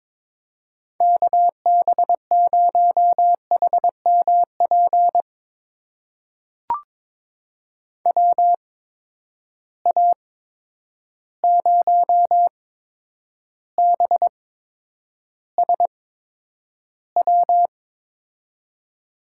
Morse code, heard as a keyed tone, about 21 wpm, spaced as for 9 wpm, 700 Hz. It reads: KB0HMP WA0BSW
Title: USA Call Signs - 6 Characters - 8x Character Spacing - 22wpm